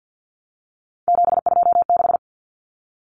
Morse code, heard as a keyed tone, about 50 wpm, 700 Hz. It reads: 726
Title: Common Cut Numbers - 3-Digit 50wpm